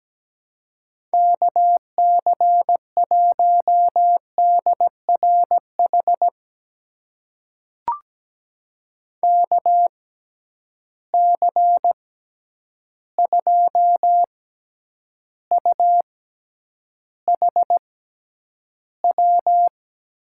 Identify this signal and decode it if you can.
Morse code, heard as a keyed tone, about 17 wpm, spaced as for 9 wpm, 700 Hz. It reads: KC1DRH KC2UHW
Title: USA Call Signs - 6 Characters - 6x Character Spacing - 17wpm